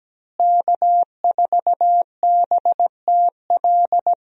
Morse code, heard as a keyed tone, about 17 wpm, 700 Hz. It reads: K4BTL
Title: USA Call Signs - 5 Characters - Repeated 3x - 17wpm